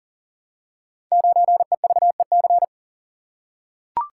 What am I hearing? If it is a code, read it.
Morse code, heard as a keyed tone, about 42 wpm, 690 Hz. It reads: 9EVEC
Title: Code Groups - 5 Letter:Numbers - 40wpm